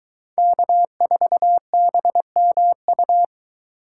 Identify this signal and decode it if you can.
Morse code, heard as a keyed tone, about 22 wpm, 700 Hz. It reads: K4BMU